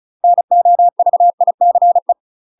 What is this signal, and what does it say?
Morse code, heard as a keyed tone, about 36 wpm, 700 Hz. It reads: NOVICE
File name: UK MASTER QSOs TRAINING / 35WPM / UK MASTER QSOs 0032